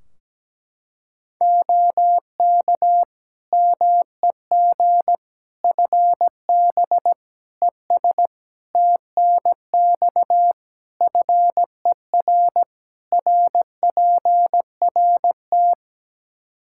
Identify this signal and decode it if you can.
Morse code, heard as a keyed tone, about 17 wpm, 700 Hz. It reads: OK MEG FB ES TNX FER RPRT
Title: Practice QSO - Exchange 2 - Info - 17wpm